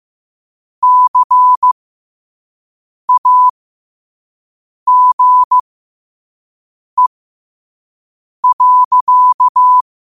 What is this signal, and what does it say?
Morse code, heard as a keyed tone, about 15 wpm, spaced as for 5 wpm, 1000 Hz. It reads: CAGE.
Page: Quotes for Sat, 16 Aug 2025 in Morse Code at 5 words per minute.